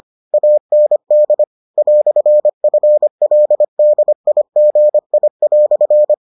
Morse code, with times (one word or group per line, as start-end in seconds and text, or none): 0.3-1.4 AND
1.8-6.1 "FLDIGI"